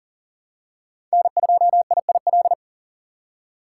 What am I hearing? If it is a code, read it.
Morse code, heard as a keyed tone, about 40 wpm, 700 Hz. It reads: N2IIL
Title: USA Call Signs - 5 Characters - 6x Character Spacing - 40wpm